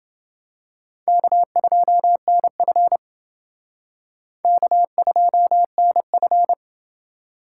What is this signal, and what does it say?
Morse code, heard as a keyed tone, about 28 wpm, 700 Hz. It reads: K2NF K2NF